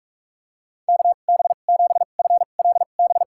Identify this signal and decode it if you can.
Morse code, heard as a keyed tone, about 44 wpm, spaced as for 32 wpm, 700 Hz. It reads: KB7FLB
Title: USA Call Signs - 6 Characters - 2x Character Spacing - 45wpm